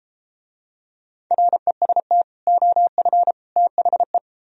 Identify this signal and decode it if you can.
Morse code, heard as a keyed tone, about 32 wpm, 700 Hz. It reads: REST OF THE